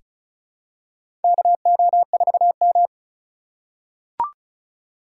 Morse code, heard as a keyed tone, about 36 wpm, 700 Hz. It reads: KO4M
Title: USA Call Signs - 8x Character Spacing - 35wpm